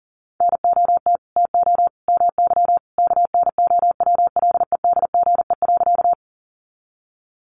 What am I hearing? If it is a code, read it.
Morse code, heard as a keyed tone, about 40 wpm, 700 Hz. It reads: NOT TO MY KNOWLEDGE.
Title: Sentences from Top 600 Words 40wpm